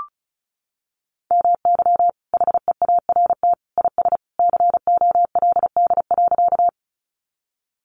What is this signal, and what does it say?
Morse code, heard as a keyed tone, about 34 wpm, 700 Hz. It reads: MY HEART IS COLD.
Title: Sentences from Top 1200 Words 35wpm